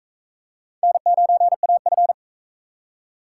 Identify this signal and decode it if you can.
Morse code, heard as a keyed tone, about 36 wpm, 700 Hz. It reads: N9AF